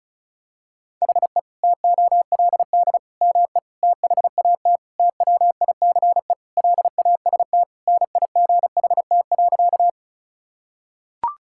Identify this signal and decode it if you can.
Morse code, heard as a keyed tone, about 36 wpm, 700 Hz. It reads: HE TOLD ME THAT TWICE LAST NIGHT.
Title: Sentences from Top 1600 Words 35wpm